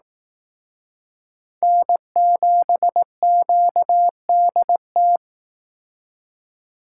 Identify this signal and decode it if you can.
Morse code, heard as a keyed tone, about 18 wpm, 700 Hz. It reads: N7QDT
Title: Speed Racing - USA 1x3 Call Signs 15wpm